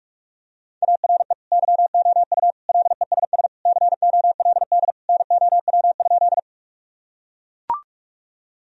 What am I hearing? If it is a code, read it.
Morse code, heard as a keyed tone, about 44 wpm, 700 Hz. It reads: ARE YOU LESS COLD NOW?
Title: Sentences from Top 1000 Words - Review Part 2 - 45wpm